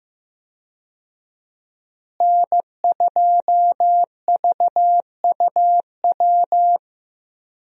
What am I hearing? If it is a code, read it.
Morse code, heard as a keyed tone, about 15 wpm, 700 Hz. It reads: N2VUW
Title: USA Call Signs - 5 Characters - Repeated 3x - 15wpm